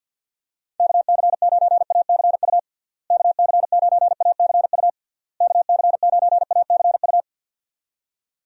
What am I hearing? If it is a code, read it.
Morse code, heard as a keyed tone, about 46 wpm, 700 Hz. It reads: KC9ACU KC9ACU KC9ACU
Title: USA Call Signs - 6 Characters - Repeated 3x - 50wpm